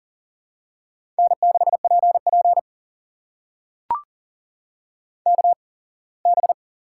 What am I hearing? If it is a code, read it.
Morse code, heard as a keyed tone, about 42 wpm, spaced as for 24 wpm, 700 Hz. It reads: N6PP KB
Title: USA Call Signs - 8x Character Spacing - 40wpm